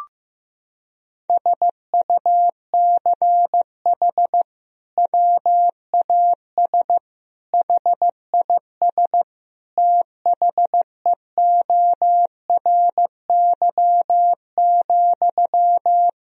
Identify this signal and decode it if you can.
Morse code, heard as a keyed tone, about 15 wpm, 700 Hz. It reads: SUCH WAS HIS THEORY,
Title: Sentences from Top 1000 Words - Review Part 4 - 15wpm